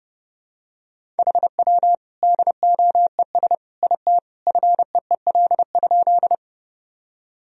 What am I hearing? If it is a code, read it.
Morse code, heard as a keyed tone, about 30 wpm, 700 Hz. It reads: HW DOES IT FEEL?
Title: Sentences from Top 400 Words and Abbreviations 30wpm